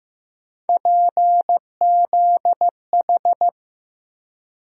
Morse code, heard as a keyed tone, about 15 wpm, 700 Hz. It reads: PZH
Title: USA Call Sign Postfixes - 3 Letters 15wpm